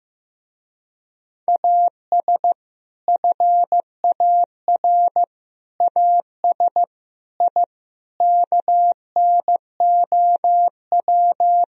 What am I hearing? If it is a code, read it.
Morse code, heard as a keyed tone, about 15 wpm, 700 Hz. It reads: AS FAR AS I KNOW
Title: Speed-Racing - Sets of 5 Words 15wpm